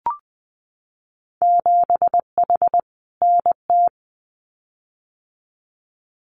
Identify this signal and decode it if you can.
Morse code, heard as a keyed tone, about 20 wpm, 700 Hz. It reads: 7H NT
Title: Winter Field Day Exchanges Encore 20wpm